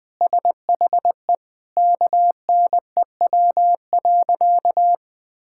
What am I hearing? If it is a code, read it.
Morse code, heard as a keyed tone, about 20 wpm, 700 Hz. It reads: SHE KNEW.
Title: Sentences with 4-Word Phrases - Spoken Beginning - 20wpm